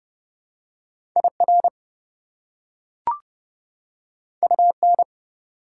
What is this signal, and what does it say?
Morse code, heard as a keyed tone, about 29 wpm, 700 Hz. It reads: IR UN
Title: Prefixes and Suffixes 30wpm